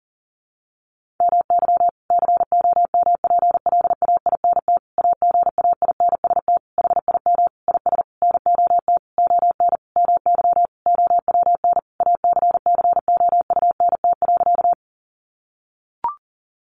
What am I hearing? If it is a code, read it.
Morse code, heard as a keyed tone, about 40 wpm, 700 Hz. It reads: MY COMPLAINT AGAINST HIM IS NOT ON MY OWN ACCOUNT.
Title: Sentences from Top 1400 Words 40wpm